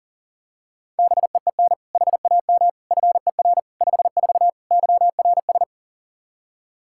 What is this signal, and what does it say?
Morse code, heard as a keyed tone, about 40 wpm, 700 Hz. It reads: BEEN HAM FER 54 YRS
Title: K1USN SST QSO Practice 40wpm